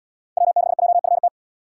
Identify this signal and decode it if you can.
Morse code, heard as a keyed tone, about 73 wpm, 700 Hz. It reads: W6JFT